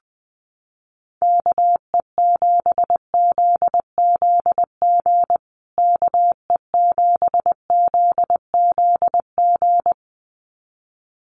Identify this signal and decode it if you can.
Morse code, heard as a keyed tone, about 20 wpm, 700 Hz. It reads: KE7ZZG KE7ZZG
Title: USA Call Signs Repeated 2x 20wpm